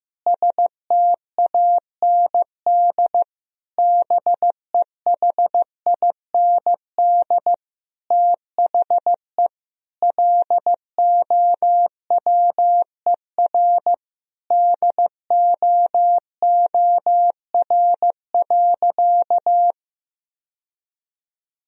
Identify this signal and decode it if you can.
Morse code, heard as a keyed tone, about 15 wpm, 700 Hz. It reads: STAND BEHIND THE LOWER DOOR.
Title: Sentences from Top 1000 Words - Review Part 4 - 15wpm